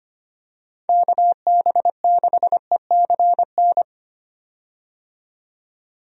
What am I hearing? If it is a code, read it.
Morse code, heard as a keyed tone, about 25 wpm, 700 Hz. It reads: KB6ECN
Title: USA Call Signs - 6 Characters - 25wpm